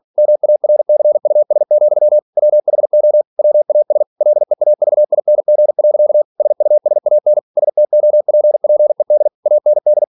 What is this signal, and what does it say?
Morse code, heard as a keyed tone, about 46 wpm, 600 Hz. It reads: MARCUS, WHO WAS LEAVING. SUSAN STOPPED AND